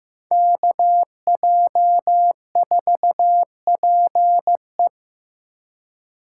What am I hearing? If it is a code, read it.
Morse code, heard as a keyed tone, about 15 wpm, 700 Hz. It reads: KJ4PE
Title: USA Call Signs - 4x Character Spacing - 15wpm